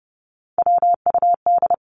Morse code, heard as a keyed tone, about 30 wpm, 700 Hz. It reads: WUD
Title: Abbreviations 30wpm